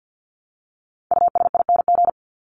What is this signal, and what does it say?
Morse code, heard as a keyed tone, about 50 wpm, 700 Hz. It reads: USING
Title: ING Suffix - Top 100 Words with Base Words 50wpm